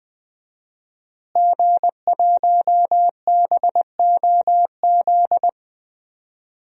Morse code, heard as a keyed tone, about 20 wpm, 700 Hz. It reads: G1BOZ
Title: DXCC-100 Call Signs - 5 Characters - Repeated 3x - 20wpm